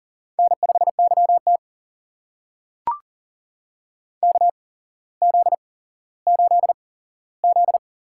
Morse code, heard as a keyed tone, about 40 wpm, spaced as for 17 wpm, 700 Hz. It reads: N5YT KZ8Z
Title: USA Call Signs - 4 Characters - 8x Character Spacing - 40wpm